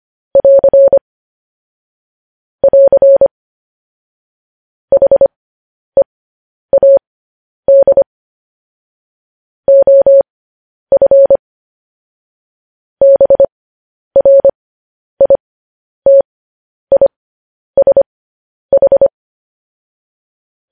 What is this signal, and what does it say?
Morse code, heard as a keyed tone, about 25 wpm, spaced as for 10 wpm, 550 Hz. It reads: <AR> <AR> HEAD OF BRITISH
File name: podcast / News headlines-1x-25x10